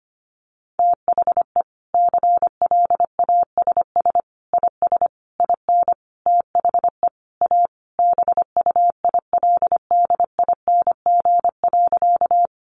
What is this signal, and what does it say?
Morse code, heard as a keyed tone, about 25 wpm, 700 Hz. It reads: THE CLASS IS IN THE A BUILDING.
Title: Additional Easy Phrases 25wpm